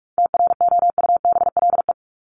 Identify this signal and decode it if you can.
Morse code, heard as a keyed tone, about 44 wpm, 700 Hz. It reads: TROUBLE
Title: Top 1001-1250 Words 45wpm